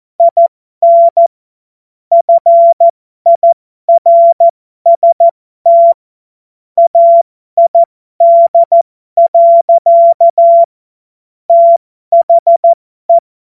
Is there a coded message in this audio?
Yes: Morse code from 0.2 to 13.2 s, heard as a keyed tone, about 14 wpm, spaced as for 12 wpm, 680 Hz. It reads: IN FIRST AID. THE